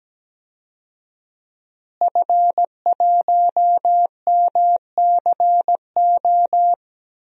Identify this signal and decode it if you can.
Morse code, heard as a keyed tone, about 17 wpm, 700 Hz. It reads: F1MCO
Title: DXCC-100 Call Signs - 5 Characters - Repeated 2x - 17wpm